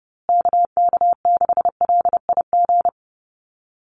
Morse code, heard as a keyed tone, about 30 wpm, 700 Hz. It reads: KK6LIG
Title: USA Call Signs Repeated 3x 30wpm